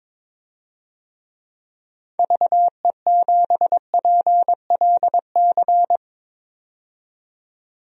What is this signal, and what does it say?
Morse code, heard as a keyed tone, about 22 wpm, 700 Hz. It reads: VE7PLC
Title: DXCC-100 Call Signs - 6 Characters - 22wpm